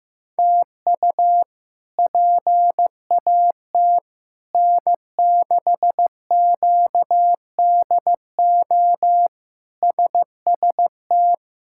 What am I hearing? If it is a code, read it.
Morse code, heard as a keyed tone, about 15 wpm, 700 Hz. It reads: TU PAT N6QDO SST
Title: K1USN SST QSO Practice - 15wpm